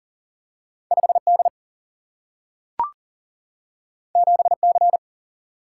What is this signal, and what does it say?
Morse code, heard as a keyed tone, about 39 wpm, 700 Hz. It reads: 5D 7C
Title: Single-Letter Single-Number 40wpm